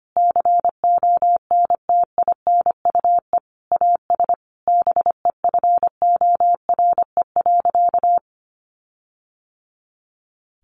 Morse code, heard as a keyed tone, about 25 wpm, 700 Hz. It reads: CONTINUE AS BEFORE.
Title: Sentences from Top 1400 Words 25wpm